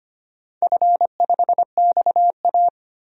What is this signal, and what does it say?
Morse code, heard as a keyed tone, about 25 wpm, 700 Hz. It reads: F5XA
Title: DXCC-100 Call Signs - 4 Characters - 25wpm